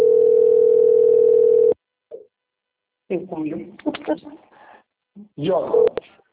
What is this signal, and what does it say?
Ringback tone (North America), 440 + 480 Hz, with speech.
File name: out- - - - - .96959